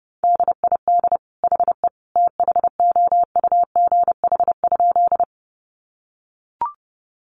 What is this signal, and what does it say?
Morse code, heard as a keyed tone, about 30 wpm, 700 Hz. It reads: DID HE THOUGH?
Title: Sentences from Top 400 Words 30wpm